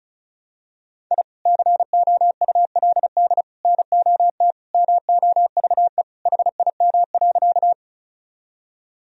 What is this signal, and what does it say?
Morse code, heard as a keyed tone, about 36 wpm, 700 Hz. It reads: I COULD NOT MOVE HIM.
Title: Sentences from Top 1000 Words - Review Part 1 - 35wpm